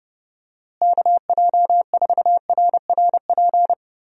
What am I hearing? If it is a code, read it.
Morse code, heard as a keyed tone, about 30 wpm, 700 Hz. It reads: KJ4RRP
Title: USA Call Signs - 6 Characters - Repeated 2x - 30wpm